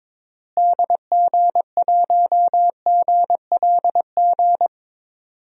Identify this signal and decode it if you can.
Morse code, heard as a keyed tone, about 22 wpm, 700 Hz. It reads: DG1GLG
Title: DXCC-100 Call Signs - 6 Characters - Repeated 2x - 22wpm